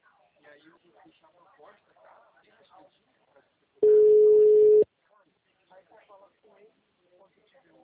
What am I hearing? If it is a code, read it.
Ringback tone (Europe, 425 Hz), with speech.